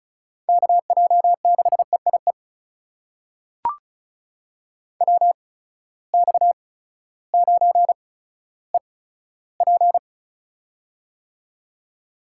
Morse code, heard as a keyed tone, about 36 wpm, spaced as for 16 wpm, 700 Hz. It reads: KJ6EIE WX9EP
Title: USA Call Signs - 8x Character Spacing - 35wpm